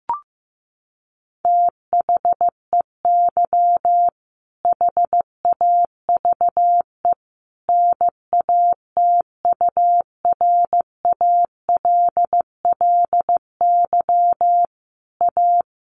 Morse code, heard as a keyed tone, about 15 wpm, 700 Hz. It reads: THEY HAVE NATURALLY A
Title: Sentences from Top 1300 Words 15wpm